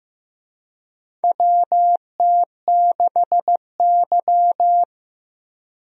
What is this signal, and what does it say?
Morse code, heard as a keyed tone, about 15 wpm, 700 Hz. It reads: WT6Y
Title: USA Call Signs - 4 Characters - Repeated 2x - 15wpm